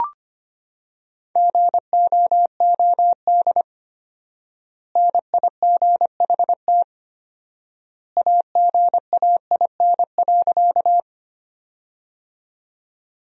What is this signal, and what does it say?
Morse code, heard as a keyed tone, about 25 wpm, 700 Hz. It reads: GOOD NIGHT AGAIN.
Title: 4x Word-Spacing - Sentences from Top 300 Words - 25wpm